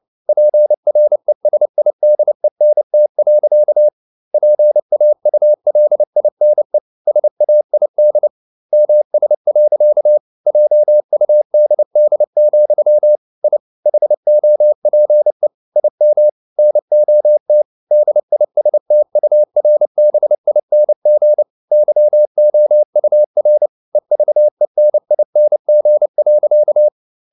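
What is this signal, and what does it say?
Morse code, heard as a keyed tone, about 29 wpm, 600 Hz. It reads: PRESIDENT. PAULINE SAID MS. JUDD, I HOPE IM NOT DISTURBING YOUR EVENING.
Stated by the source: Morse code practice at 28 wpm, text of Ken Follett's novel Never in English, from CW med Gnister.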